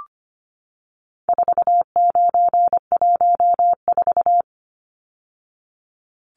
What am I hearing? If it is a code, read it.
Morse code, heard as a keyed tone, about 25 wpm, 700 Hz. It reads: 4914
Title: Numbers - 4-Digit 25wpm